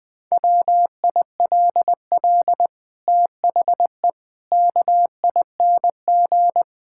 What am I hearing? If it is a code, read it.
Morse code, heard as a keyed tone, about 20 wpm, 700 Hz. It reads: WILL THE KING
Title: Sentences from Top 1000 Words - Review Part 6 - 20wpm